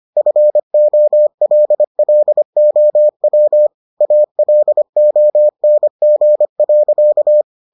Morse code, heard as a keyed tone, about 25 wpm, 600 Hz. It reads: FOLLOW ALONG.